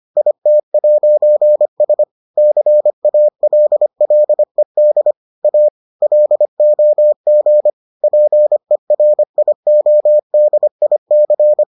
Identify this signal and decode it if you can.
Morse code, heard as a keyed tone, about 25 wpm, 600 Hz. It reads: IT'S CALLED A LOG PERIODIC